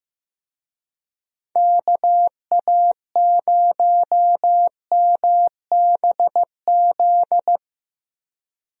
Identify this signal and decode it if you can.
Morse code, heard as a keyed tone, about 16 wpm, 700 Hz. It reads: KA0MBZ
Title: USA Call Signs - 6 Characters - Repeated 2x - 15wpm